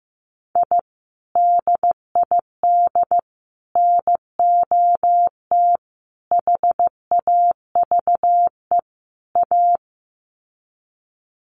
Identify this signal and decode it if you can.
Morse code, heard as a keyed tone, about 15 wpm, 700 Hz. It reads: I DID NOT HAVE A
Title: Sets of 5 Words 15wpm